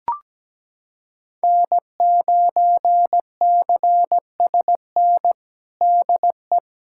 Morse code, heard as a keyed tone, about 17 wpm, 700 Hz. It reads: N9CSN DE
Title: USA Callsign DE Callsign 17wpm